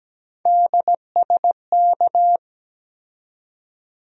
Morse code, heard as a keyed tone, about 17 wpm, 700 Hz. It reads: DSK
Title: USA Call Sign Postfixes - 3 Letters 17wpm